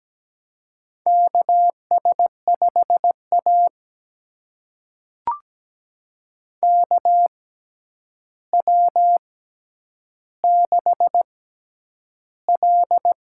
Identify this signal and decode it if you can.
Morse code, heard as a keyed tone, about 17 wpm, spaced as for 9 wpm, 700 Hz. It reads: KS5A KW6L
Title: USA Call Signs - 4 Characters - 6x Character Spacing - 17wpm